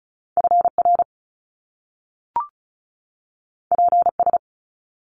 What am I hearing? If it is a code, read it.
Morse code, heard as a keyed tone, about 36 wpm, 700 Hz. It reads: FR PS
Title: Top Two Letter Combinations in Words 35wpm